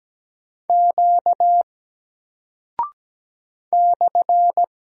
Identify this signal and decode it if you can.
Morse code, heard as a keyed tone, about 17 wpm, 700 Hz. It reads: Q /
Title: Single Character (Letter:Number:Punctuation:Prosign) 17wpm